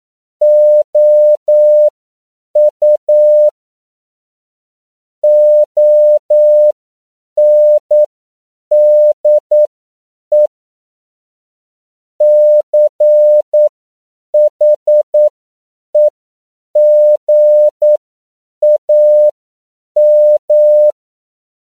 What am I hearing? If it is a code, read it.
Morse code, heard as a keyed tone, about 9 wpm, spaced as for 7 wpm, 600 Hz. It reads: OU ONDE CHEGAM
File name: CW / 10PPM